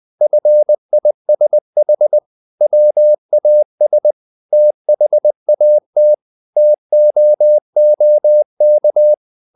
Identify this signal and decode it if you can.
Morse code, heard as a keyed tone, about 20 wpm, 600 Hz. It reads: FISH WAS THAT TOOK
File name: the old man and the sea / the old man and the sea part 1 20wpm